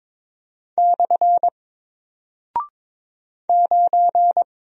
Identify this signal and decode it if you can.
Morse code, heard as a keyed tone, about 22 wpm, 700 Hz. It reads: / 9
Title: Single Character (Letter:Number:Punctuation:Prosign) Alt 22wpm